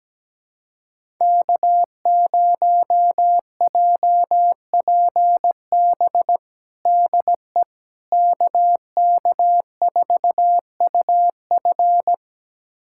Morse code, heard as a keyed tone, about 17 wpm, 700 Hz. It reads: K0JPB DE KK4UF
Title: USA Callsign DE Callsign - 5 Characters - 17wpm